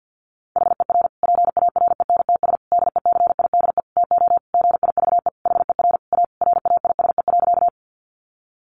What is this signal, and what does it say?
Morse code, heard as a keyed tone, about 48 wpm, 700 Hz. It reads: HER PARENTS DECIDE TO GIVE HER A RAISE.
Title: Easy Phrases 50wpm